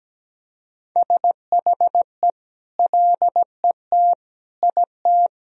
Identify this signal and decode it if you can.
Morse code, heard as a keyed tone, about 17 wpm, 700 Hz. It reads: SHE LET IT
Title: Sentences from Top 1000 Words - Review Part 1 - 17wpm